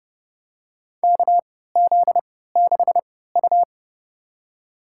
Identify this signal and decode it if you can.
Morse code, heard as a keyed tone, about 30 wpm, spaced as for 17 wpm, 700 Hz. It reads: KZ6U
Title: USA Call Signs - 4 Characters - 3x Character Spacing - 30wpm